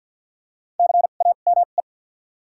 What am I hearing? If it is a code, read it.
Morse code, heard as a keyed tone, about 53 wpm, spaced as for 37 wpm, 700 Hz. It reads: CAME
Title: 2x Character Spacing - Top 300 Words - 50wpm